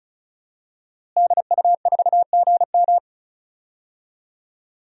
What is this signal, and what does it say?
Morse code, heard as a keyed tone, about 36 wpm, 700 Hz. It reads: DU4GM
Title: DXCC-100 Call Signs - 5 Characters - Repeated 3x - 35wpm